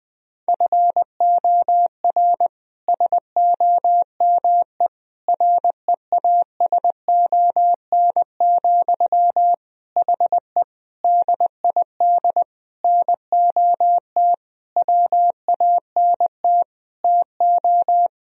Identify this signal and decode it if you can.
Morse code, heard as a keyed tone, about 20 wpm, 700 Hz. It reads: FOR SOME REASON, HE DID NOT WANT TO
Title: Sentences from Top 1000 Words - Review Part 3 - 20wpm